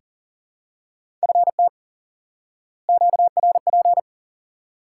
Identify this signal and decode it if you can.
Morse code, heard as a keyed tone, about 40 wpm, 700 Hz. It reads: FT QRP
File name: IQE Course 3 - Lesson 89 - YAGI - Two Elements - 40wpm